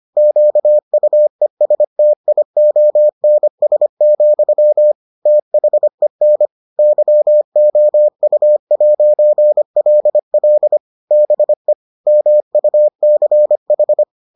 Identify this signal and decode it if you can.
Morse code, heard as a keyed tone, about 25 wpm, 600 Hz. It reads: QUESTIONS, THEN YOU'LL BE MUCH